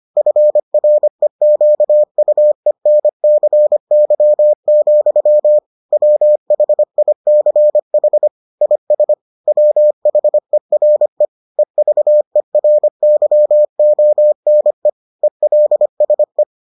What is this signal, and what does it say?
Morse code, heard as a keyed tone, about 25 wpm, 600 Hz. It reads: FREQUENCY, WHICH IS WHERE EVERYONE ELSE